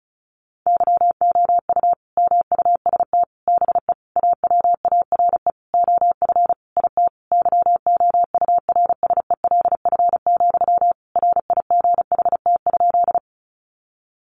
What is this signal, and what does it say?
Morse code, heard as a keyed tone, about 34 wpm, 700 Hz. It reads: YOU MUST BE AWARE OF IT YOURSELF, RIGHT?
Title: Sentences from Top 1200 Words 35wpm